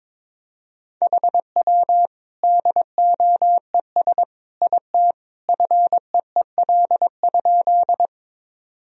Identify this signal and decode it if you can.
Morse code, heard as a keyed tone, about 22 wpm, 700 Hz. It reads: HW DOES IT FEEL?
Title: Sentences from Top 400 Words and Abbreviations 22wpm